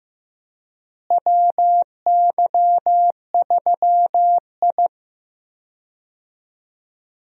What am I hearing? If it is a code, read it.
Morse code, heard as a keyed tone, about 16 wpm, 700 Hz. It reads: WY3I
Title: USA Call Signs - 4 Characters - Repeated 3x - 15wpm